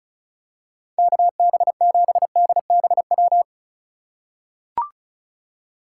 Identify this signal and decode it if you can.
Morse code, heard as a keyed tone, about 36 wpm, 700 Hz. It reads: KB7DBW
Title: USA Call Signs - 6 Characters - 3x Character Spacing - 35wpm